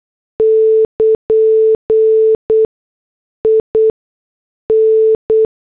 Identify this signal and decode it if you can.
Morse code, heard as a keyed tone, about 8 wpm, spaced as for 6 wpm, 440 Hz. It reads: <KN> IN
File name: morse / headlines